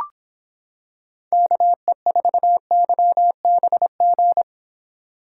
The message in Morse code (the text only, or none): KE4YBG